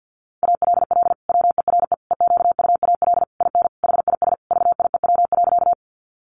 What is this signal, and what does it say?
Morse code, heard as a keyed tone, about 50 wpm, 700 Hz. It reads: ALL WERE EQUAL IN HIS VIEW.
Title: Sentences from Top 1400 Words 50wpm